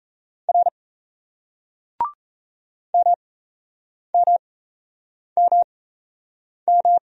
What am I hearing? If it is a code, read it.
Morse code, heard as a keyed tone, about 35 wpm, spaced as for 8 wpm, 700 Hz. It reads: R MMMM